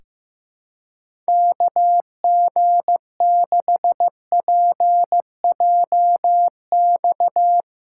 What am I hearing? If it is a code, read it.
Morse code, heard as a keyed tone, about 15 wpm, 700 Hz. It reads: KG6PJX
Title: USA Callsign DE Callsign - 6 Characters - 15wpm